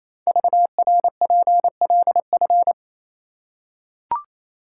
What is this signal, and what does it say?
Morse code, heard as a keyed tone, about 27 wpm, 700 Hz. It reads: VRPLF
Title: Code Groups - 5 Letters - 28wpm